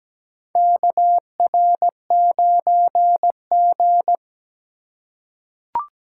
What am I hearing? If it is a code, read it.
Morse code, heard as a keyed tone, about 17 wpm, 700 Hz. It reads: KR9G
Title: USA Call Signs - 4x Character Spacing - 17wpm